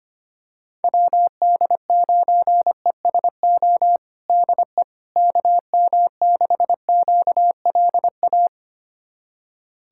Morse code, heard as a keyed tone, about 25 wpm, 700 Hz. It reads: WD9ESO DE KM6QLA
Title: USA Callsign DE Callsign 25wpm